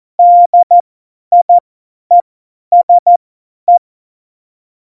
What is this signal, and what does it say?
Morse code, heard as a keyed tone, about 14 wpm, spaced as for 10 wpm, 700 Hz. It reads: DIESE
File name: Morsekurs / Witze / Witz 03.Juli 2018, 50 F70